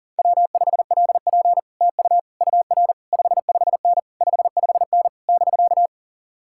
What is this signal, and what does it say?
Morse code, heard as a keyed tone, about 49 wpm, 700 Hz. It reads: W5LP TU UR 55N 55N <BK>
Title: Speed-Racing Common QSO Phrases - 40wpm